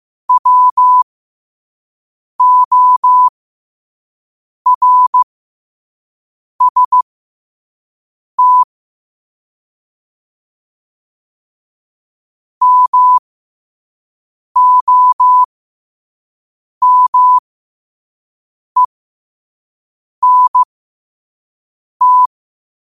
Morse code, heard as a keyed tone, about 15 wpm, spaced as for 5 wpm, 1000 Hz. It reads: WORST MOMENT
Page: Quotes for Fri, 15 Aug 2025 in Morse Code at 5 words per minute.